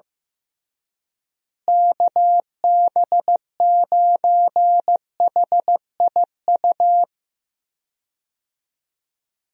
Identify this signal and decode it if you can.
Morse code, heard as a keyed tone, about 15 wpm, 700 Hz. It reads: KB9HIU
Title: USA Call Signs - 6 Characters - Repeated 3x - 15wpm